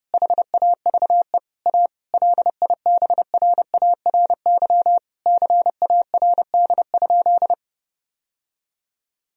Morse code, heard as a keyed tone, about 30 wpm, 700 Hz. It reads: HAVE A LIBRARY CARD?
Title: Sentences from Top 1000 Words - Review Part 4 - 30wpm